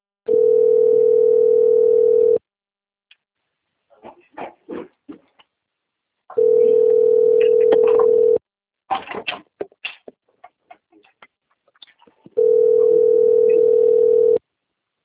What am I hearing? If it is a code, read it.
Ringback tone (North America), 440 + 480 Hz.